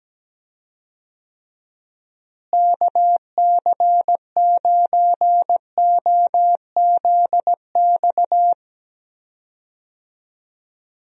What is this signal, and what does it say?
Morse code, heard as a keyed tone, about 17 wpm, 700 Hz. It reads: KC9OZX
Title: USA Call Signs - 6 Characters - Repeated 2x - 17wpm